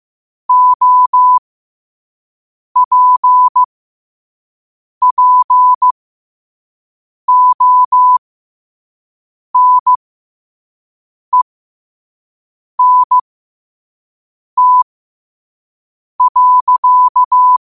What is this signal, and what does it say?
Morse code, heard as a keyed tone, about 15 wpm, spaced as for 5 wpm, 1000 Hz. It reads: OPPONENT.
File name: qotdfiles / 5wpm